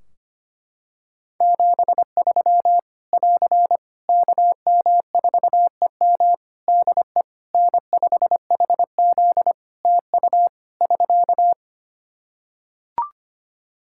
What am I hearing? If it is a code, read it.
Morse code, heard as a keyed tone, about 25 wpm, 700 Hz. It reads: 73 <AR> KM4EM DE N5HZ TU <SK>
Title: Practice QSO - Exchange 4 - Ending - 25wpm